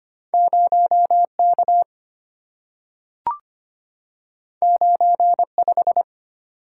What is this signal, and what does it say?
Morse code, heard as a keyed tone, about 25 wpm, 700 Hz. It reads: 0K 95
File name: Two Letter-Number Combinations - 25wpm